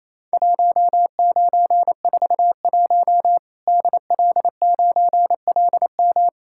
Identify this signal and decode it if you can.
Morse code, heard as a keyed tone, about 28 wpm, 700 Hz. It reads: 1941 DL9LM
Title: Worked All Europe DX Contest - QTC Exchange - 28wpm